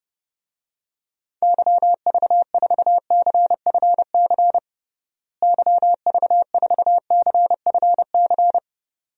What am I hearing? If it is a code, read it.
Morse code, heard as a keyed tone, about 30 wpm, 700 Hz. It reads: YV4CFC YV4CFC
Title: DXCC-100 Call Signs - 6 Characters - Repeated 2x - 30wpm